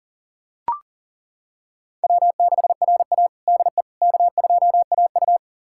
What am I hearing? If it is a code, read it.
Morse code, heard as a keyed tone, about 40 wpm, 700 Hz. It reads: W6RA DE K2AU
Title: USA Callsign DE Callsign - 4 Characters - 40wpm